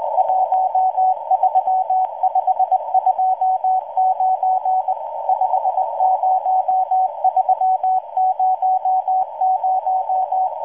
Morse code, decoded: 13529 50300